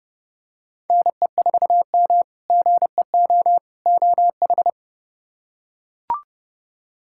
Morse code, heard as a keyed tone, about 30 wpm, 700 Hz. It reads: NE4M GEO OH
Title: K1USN SST QSO Practice - 30wpm